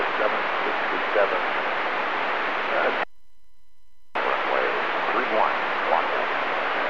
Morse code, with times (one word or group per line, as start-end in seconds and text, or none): none